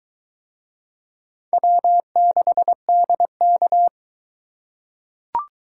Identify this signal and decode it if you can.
Morse code, heard as a keyed tone, about 22 wpm, 700 Hz. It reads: W6DK